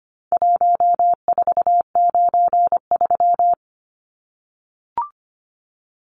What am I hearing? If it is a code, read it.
Morse code, heard as a keyed tone, about 25 wpm, 700 Hz. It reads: 1493
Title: Numbers - Contest Style - 4-Digit 25wpm